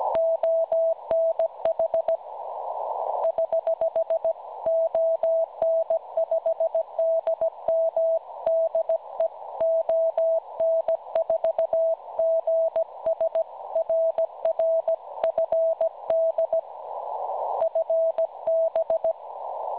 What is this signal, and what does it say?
Morse code, heard as a keyed tone, about 16 wpm, 650 Hz. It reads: ONH <HH> ON5DM DE ON4GS RR FD FB
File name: cwronde / websdr recording .6kHz